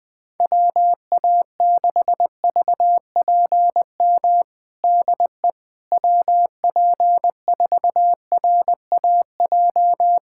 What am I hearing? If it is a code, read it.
Morse code, heard as a keyed tone, about 20 wpm, 700 Hz. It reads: WA6VPM DE WP4RAJ
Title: USA Callsign DE Callsign - 6 Characters - 20wpm